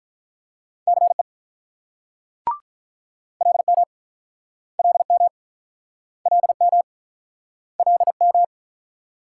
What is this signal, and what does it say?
Morse code, heard as a keyed tone, about 46 wpm, 700 Hz. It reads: CE LM LM LM LM